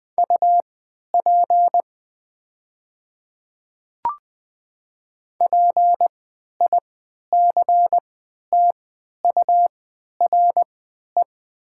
Morse code, heard as a keyed tone, about 20 wpm, spaced as for 11 wpm, 700 Hz. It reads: UP PICTURE
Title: 3x Character Spacing - Top 300 Words - 20wpm